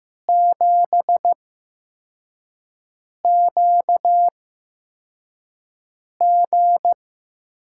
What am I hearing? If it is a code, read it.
Morse code, heard as a keyed tone, about 15 wpm, spaced as for 4 wpm, 700 Hz. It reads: 7QG
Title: USA Call Signs - 6 Characters - 8x Character Spacing - 15wpm